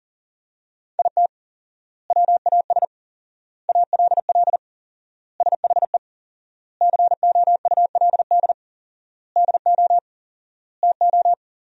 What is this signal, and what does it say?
Morse code, heard as a keyed tone, about 40 wpm, 700 Hz. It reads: IT WAS ALL SHE COULD DO TO
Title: 4x Word-Spacing - Sentences from Top 400 Words - 40wpm